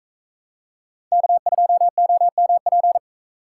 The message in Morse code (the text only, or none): K2OMP